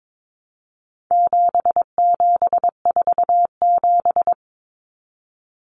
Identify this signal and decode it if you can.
Morse code, heard as a keyed tone, about 22 wpm, 700 Hz. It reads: 7747
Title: Common Cut Numbers - 4-Digit 22wpm